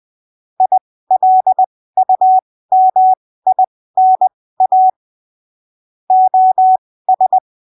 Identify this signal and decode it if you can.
Morse code, heard as a keyed tone, about 20 wpm, spaced as for 15 wpm, 750 Hz. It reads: ILUMINA OS
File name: arquivos / texto / texto 29 20ppm